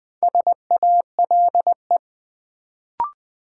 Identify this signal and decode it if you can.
Morse code, heard as a keyed tone, about 20 wpm, 700 Hz. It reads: SALE
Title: Build A Word - 2 Letters At A Time - 4-Letter Words - 20wpm